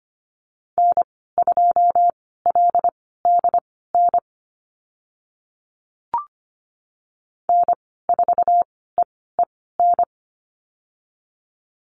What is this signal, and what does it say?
Morse code, heard as a keyed tone, about 25 wpm, spaced as for 16 wpm, 690 Hz. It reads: N2LDN N4EEN
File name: Lesson 15g - Number 5 - Call Signs 25wpm